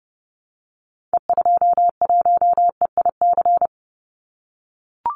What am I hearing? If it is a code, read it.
Morse code, heard as a keyed tone, about 30 wpm, 700 Hz. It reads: E21EIC
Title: Call Signs - 30wpm